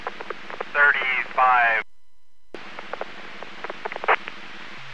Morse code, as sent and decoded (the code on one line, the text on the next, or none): none